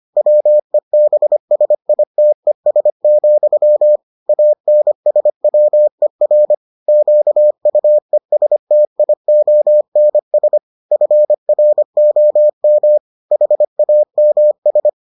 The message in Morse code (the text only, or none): WEBSITES, ANSWER QUESTIONS FROM HAMS